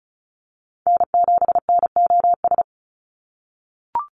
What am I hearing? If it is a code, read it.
Morse code, heard as a keyed tone, about 34 wpm, 700 Hz. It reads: N7NOS
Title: USA Call Signs 35wpm